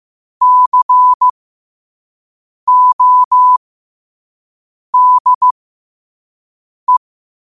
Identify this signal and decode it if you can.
Morse code, heard as a keyed tone, about 15 wpm, spaced as for 5 wpm, 1000 Hz. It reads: CODE
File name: qotdfiles / 5wpm